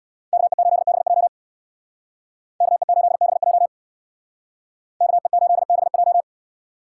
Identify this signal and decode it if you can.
Morse code, heard as a keyed tone, about 62 wpm, 700 Hz. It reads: KE8BJ KE8BJ KE8BJ
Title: Speed Racing - USA 2x2 Call Signs 50wpm